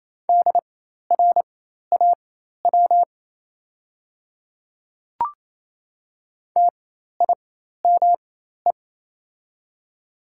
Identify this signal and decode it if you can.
Morse code, heard as a keyed tone, about 28 wpm, spaced as for 13 wpm, 700 Hz. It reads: DRAW TIME
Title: 4x Character Spacing - Top 300 Words - 28wpm